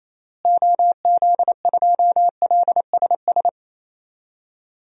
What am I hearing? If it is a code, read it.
Morse code, heard as a keyed tone, about 28 wpm, 700 Hz. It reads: OZ2LSS
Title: DXCC-100 Call Signs - 6 Characters - Repeated 3x - 28wpm